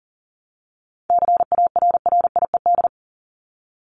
Morse code, heard as a keyed tone, about 40 wpm, 700 Hz. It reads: CARRIED
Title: ED Suffix - Top 100 Words with Base Words 40wpm